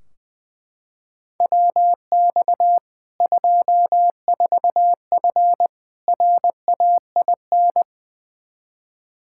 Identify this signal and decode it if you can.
Morse code, heard as a keyed tone, about 21 wpm, 700 Hz. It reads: WX 24F RAIN
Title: K1USN SST QSO Practice 20wpm